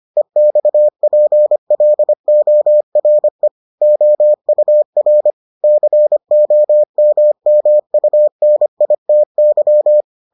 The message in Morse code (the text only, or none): EXPLORE OUR COMMUNITY